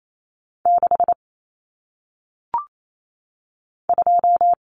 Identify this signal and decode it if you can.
Morse code, heard as a keyed tone, about 28 wpm, 700 Hz. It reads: 6 2
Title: Numbers - 1-Digit 28wpm